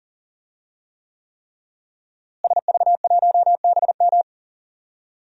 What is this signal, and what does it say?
Morse code, heard as a keyed tone, about 40 wpm, 700 Hz. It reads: SV1BM
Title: DXCC-100 Call Signs - 5 Characters - Repeated 3x - 40wpm